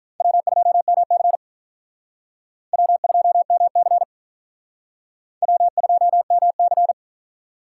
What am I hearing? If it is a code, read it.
Morse code, heard as a keyed tone, about 43 wpm, 700 Hz. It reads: W2MC W2MC W2MC
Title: Speed Racing - USA 1x2 Call Signs 35wpm